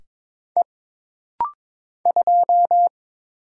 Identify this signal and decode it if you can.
Morse code, heard as a keyed tone, about 22 wpm, 700 Hz. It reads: E 2
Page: This rapid-fire timing is geared to push your ICR (Instant Character Recognition) skills.